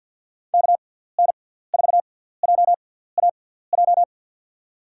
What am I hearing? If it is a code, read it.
Morse code, heard as a keyed tone, about 52 wpm, spaced as for 17 wpm, 700 Hz. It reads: KN4JAJ
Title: USA Call Signs - 6 Characters - 6x Character Spacing - 50wpm